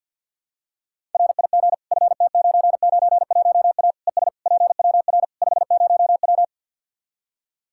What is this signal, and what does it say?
Morse code, heard as a keyed tone, about 45 wpm, 700 Hz. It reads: RIG FT991A ES PWR 50W
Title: Practice QSO - Exchange 2 - Info - 50wpm